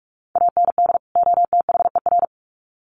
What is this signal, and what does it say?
Morse code, heard as a keyed tone, about 46 wpm, 700 Hz. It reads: AND OTHER
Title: Sets of 2 Words 45wpm